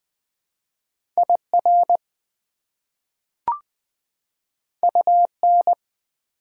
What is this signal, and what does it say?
Morse code, heard as a keyed tone, about 20 wpm, 700 Hz. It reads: IR UN
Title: Prefixes and Suffixes 20wpm